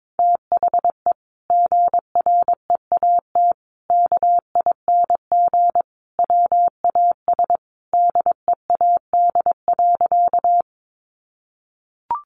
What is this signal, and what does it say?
Morse code, heard as a keyed tone, about 22 wpm, 700 Hz. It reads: THE GREAT KING WAS DEAD.
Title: Sentences from Top 1500 Words 22wpm